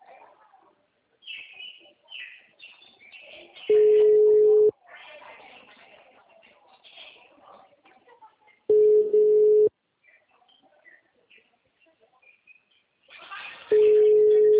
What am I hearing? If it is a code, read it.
Ringback tone (Europe, 425 Hz), with speech.